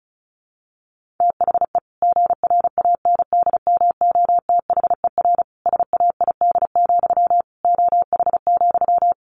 Morse code, heard as a keyed tone, about 34 wpm, 700 Hz. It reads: THE GRANDMOTHER SAID, OH,
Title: Sentences from Top 1200 Words 35wpm